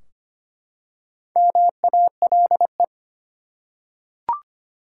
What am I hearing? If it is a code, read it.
Morse code, heard as a keyed tone, about 25 wpm, 700 Hz. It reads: MALE
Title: Four Letter Words - 25wpm